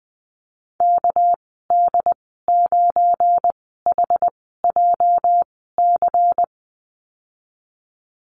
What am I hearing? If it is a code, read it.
Morse code, heard as a keyed tone, about 19 wpm, spaced as for 14 wpm, 700 Hz. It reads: KD9HJC
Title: 245 Lesson 39g - Letter z - Call Signs 20wpm